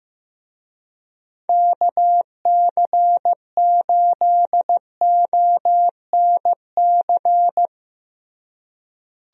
Morse code, heard as a keyed tone, about 15 wpm, 700 Hz. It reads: KC8ONC
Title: USA Call Signs - 6 Characters - Repeated 3x - 15wpm